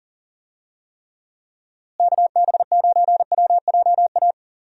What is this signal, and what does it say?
Morse code, heard as a keyed tone, about 39 wpm, 700 Hz. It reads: KB9WJA
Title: USA Call Signs - 6 Characters - Repeated 3x - 40wpm